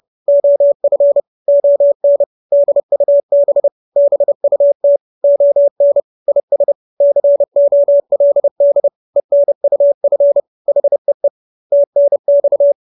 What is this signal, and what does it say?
Morse code, heard as a keyed tone, about 30 wpm, 570 Hz. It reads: OF ON DUB BUT ON IS COLD ENUF HEE TNX